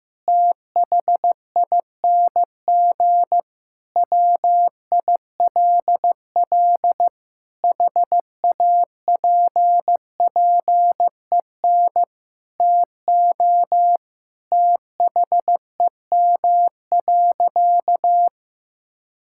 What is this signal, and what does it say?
Morse code, heard as a keyed tone, about 15 wpm, 700 Hz. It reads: THING WILL HAPPEN TO THEM.
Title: Sentences from Top 1000 Words - Review Part 5 - 15wpm